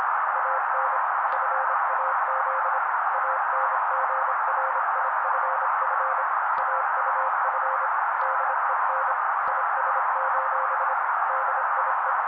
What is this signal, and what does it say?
Morse code, heard as a keyed tone, about 25 wpm, 530 Hz. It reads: AN FAZ ANGRIFF AUF DEN IS: DIE
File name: daily / 25wpm noise